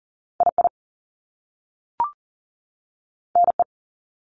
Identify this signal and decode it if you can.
Morse code, heard as a keyed tone, about 39 wpm, spaced as for 3 wpm, 700 Hz. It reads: I IN E
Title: Top Two Letter Combinations in Words 40wpm